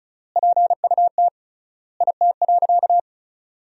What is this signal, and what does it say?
Morse code, heard as a keyed tone, about 36 wpm, 700 Hz. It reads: PUT IT.
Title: 3x Word-Spacing - Sentences from Top 200 Words - 35wpm